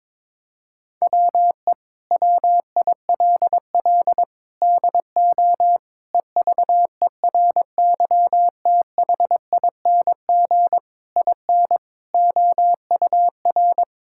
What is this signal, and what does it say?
Morse code, heard as a keyed tone, about 22 wpm, 700 Hz. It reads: WE WILL DO EVERYTHING IN OUR
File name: practice / 22wpm / Sets of 8 Words - 22wpm